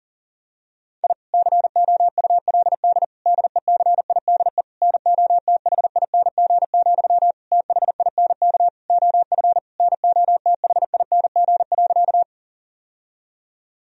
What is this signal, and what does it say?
Morse code, heard as a keyed tone, about 40 wpm, 700 Hz. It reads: I COULD DECIDE NOTHING, THINK OF NOTHING.
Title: Sentences from Top 1000 Words - Review Part 3 - 40wpm